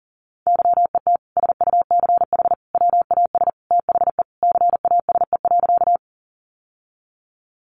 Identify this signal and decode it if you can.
Morse code, heard as a keyed tone, about 40 wpm, 700 Hz. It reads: YET SUCH WAS THE CASE.
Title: Sentences from Top 700 Words 40wpm